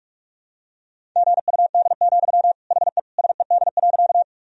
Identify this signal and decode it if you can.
Morse code, heard as a keyed tone, about 46 wpm, 700 Hz. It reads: GUD, HE SED.
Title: Sentences from Top 200 Words and Abbreviations 45wpm